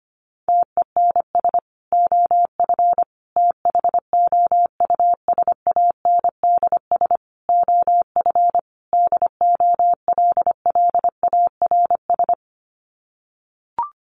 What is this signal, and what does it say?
Morse code, heard as a keyed tone, about 25 wpm, 700 Hz. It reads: TENS OF THOUSANDS OF DOLLARS
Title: Sets of 5 Words Encore2 25wpm